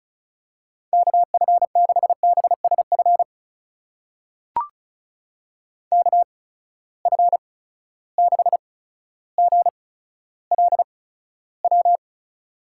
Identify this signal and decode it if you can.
Morse code, heard as a keyed tone, about 36 wpm, spaced as for 15 wpm, 700 Hz. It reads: KF6BSF KF6GLW
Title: USA Call Signs - 6 Characters - 8x Character Spacing - 35wpm